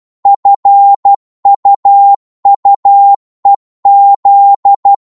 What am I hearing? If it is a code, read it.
Morse code, heard as a keyed tone, about 12 wpm, 800 Hz. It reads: FUUEZ
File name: Morse / ltr-12wpm